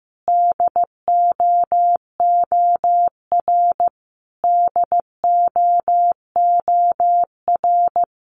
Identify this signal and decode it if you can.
Morse code, heard as a keyed tone, about 15 wpm, 700 Hz. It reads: DOOR DOOR
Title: Top 500 Words Repeated 15wpm